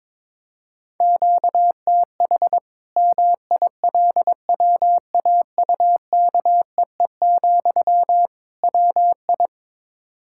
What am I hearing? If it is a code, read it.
Morse code, heard as a keyed tone, about 22 wpm, 700 Hz. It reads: QTH MILWAUKEE, WI
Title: Practice QSO - Exchange 1 - Beginning - 22wpm